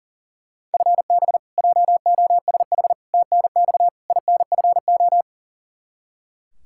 Morse code, heard as a keyed tone, about 40 wpm, 700 Hz. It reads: FB JOSH TNX INFO
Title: Practice QSO - Exchange 3 - More Info - 40wpm